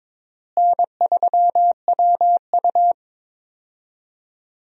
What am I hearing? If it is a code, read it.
Morse code, heard as a keyed tone, about 22 wpm, 700 Hz. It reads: N3WU
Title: USA Call Signs - 4 Characters - Repeated 2x - 22wpm